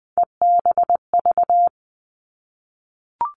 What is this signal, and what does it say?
Morse code, heard as a keyed tone, about 20 wpm, 700 Hz. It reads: EBV
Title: Three Challenging Letter-Number Combinations 20wpm